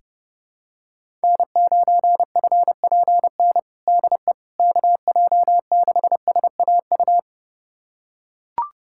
Morse code, heard as a keyed tone, about 30 wpm, 700 Hz. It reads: N9FPN DE KJ6SAU
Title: USA Callsign DE Callsign 30wpm